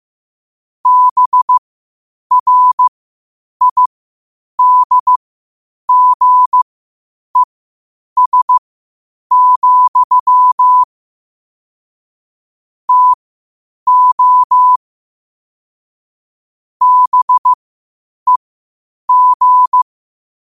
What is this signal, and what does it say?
Morse code, heard as a keyed tone, about 15 wpm, spaced as for 8 wpm, 1000 Hz. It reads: BRIDGES, TO BEG
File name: qotdfiles / 8wpm